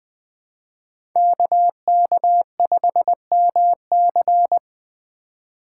Morse code, heard as a keyed tone, about 25 wpm, 700 Hz. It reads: KK5MC
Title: Speed Racing - USA 2x2 Call Signs 20wpm